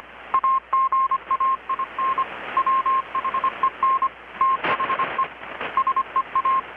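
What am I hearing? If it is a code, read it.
Morse code, heard as a keyed tone, about 25 wpm, 1060 Hz, with radio noise. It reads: AGAIN WHEN THE SEA
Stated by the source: Morse from KPH's Night of Nights XX maritime radiotelegraphy event, received in Chiapas, Mexico about 3,800 km away on a Kenwood R-600 with a grounded 10 m long-wire antenna.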